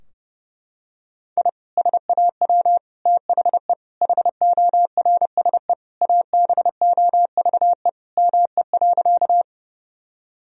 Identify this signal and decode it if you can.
Morse code, heard as a keyed tone, about 30 wpm, 700 Hz. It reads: I SAW THE HORSE ABOVE ME.
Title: Sentences from Top 1000 Words - Review Part 2 - 30wpm